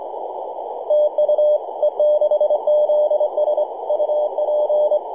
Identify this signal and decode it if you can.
Morse code, heard as a keyed tone, about 25 wpm, 600 Hz, with noise. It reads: TU E6ZS UP
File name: e6zs-21-009mhz-cw